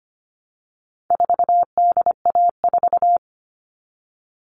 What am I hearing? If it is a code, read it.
Morse code, heard as a keyed tone, about 25 wpm, 700 Hz. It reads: 4DA4
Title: All Cut Numbers - 4-Digit 25wpm